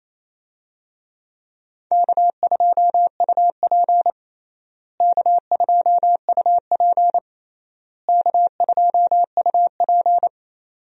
Morse code, heard as a keyed tone, about 28 wpm, 700 Hz. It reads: K2UP K2UP K2UP
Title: USA Call Signs - 4 Characters - Repeated 3x - 28wpm